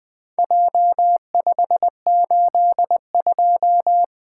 Morse code, heard as a keyed tone, about 20 wpm, 700 Hz. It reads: J582
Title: Four Letter-Number Combinations - 20wpm